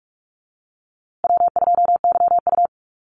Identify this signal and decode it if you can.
Morse code, heard as a keyed tone, about 44 wpm, 700 Hz. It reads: W2YU
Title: USA Call Signs Repeated 3x 45wpm